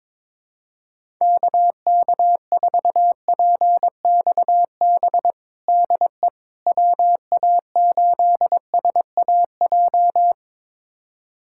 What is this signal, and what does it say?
Morse code, heard as a keyed tone, about 22 wpm, 700 Hz. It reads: KK4PXB DE WA8SAJ
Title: USA Callsign DE Callsign - 6 Characters - 22wpm